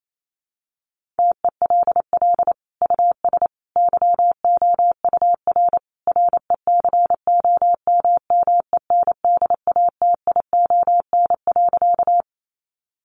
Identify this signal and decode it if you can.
Morse code, heard as a keyed tone, about 28 wpm, 700 Hz. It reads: TELL US YOUR RECOMMENDATION.
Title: Sentences from Top 1400 Words 28wpm